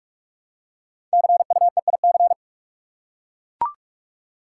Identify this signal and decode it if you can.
Morse code, heard as a keyed tone, about 44 wpm, 690 Hz. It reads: CUEIC
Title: Code Groups - 5 Letter:Numbers - 45wpm